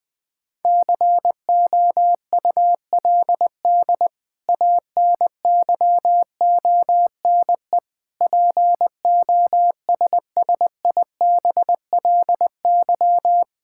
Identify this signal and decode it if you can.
Morse code, heard as a keyed tone, about 20 wpm, 700 Hz. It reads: COULD ANYONE POSSIBLY
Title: Sentences from Top 1600 Words 20wpm